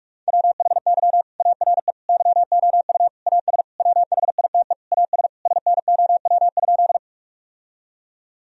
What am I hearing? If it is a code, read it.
Morse code, heard as a keyed tone, about 45 wpm, 700 Hz. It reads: WHY ARE YOU AS WHITE AS SNOW?
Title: Sentences from Top 1000 Words - Review Part 3 - 45wpm